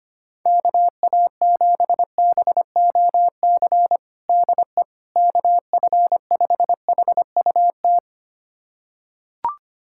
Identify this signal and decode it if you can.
Morse code, heard as a keyed tone, about 25 wpm, 700 Hz. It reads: KA7BOC DE KF5HUT
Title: USA Callsign DE Callsign - 6 Characters - 25wpm